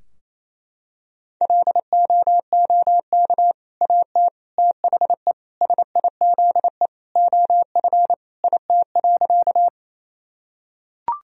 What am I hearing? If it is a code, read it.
Morse code, heard as a keyed tone, about 27 wpm, 700 Hz. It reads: LOOK AT THE SIZE OF IT.
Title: Sentences from Top 1000 Words - Review Part 2 - 28wpm